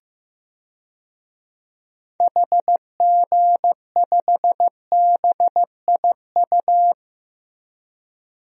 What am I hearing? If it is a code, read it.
Morse code, heard as a keyed tone, about 15 wpm, 700 Hz. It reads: HG5BIU
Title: DXCC-100 Call Signs - 6 Characters - Repeated 3x - 15wpm